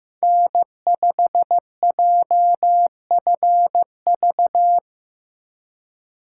Morse code, heard as a keyed tone, about 16 wpm, 700 Hz. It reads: N5JFV
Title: USA Call Signs - 5 Characters - 2x Character Spacing - 15wpm